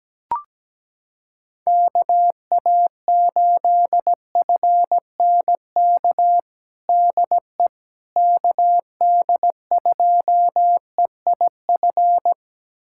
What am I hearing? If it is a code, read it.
Morse code, heard as a keyed tone, about 17 wpm, 700 Hz. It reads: KA8FNK DE KD2EIF
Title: USA Callsign DE Callsign - 6 Characters - 17wpm